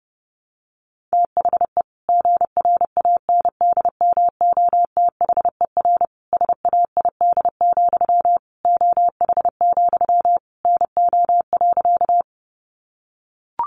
Morse code, heard as a keyed tone, about 30 wpm, 700 Hz. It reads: THE GRANDMOTHER SAID, OH, NO.
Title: Sentences from Top 1200 Words 30wpm